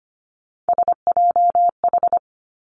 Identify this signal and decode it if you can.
Morse code, heard as a keyed tone, about 25 wpm, 700 Hz. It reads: SJH
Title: Three Challenging Letter-Number Combinations 25wpm